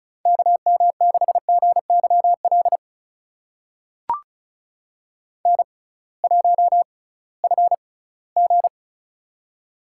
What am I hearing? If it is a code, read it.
Morse code, heard as a keyed tone, about 36 wpm, spaced as for 20 wpm, 700 Hz. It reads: KM6GYL N1FG
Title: USA Call Signs - 6x Character Spacing - 35wpm